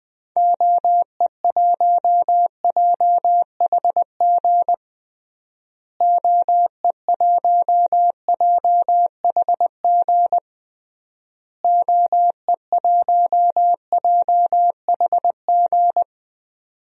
Morse code, heard as a keyed tone, about 20 wpm, 700 Hz. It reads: OE1JHG OE1JHG OE1JHG
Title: DXCC-100 Call Signs - 6 Characters - Repeated 3x - 20wpm